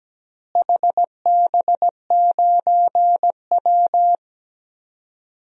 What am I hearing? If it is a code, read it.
Morse code, heard as a keyed tone, about 17 wpm, 700 Hz. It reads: HB9W
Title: DXCC-100 Call Signs - 4 Characters - Repeated 2x - 17wpm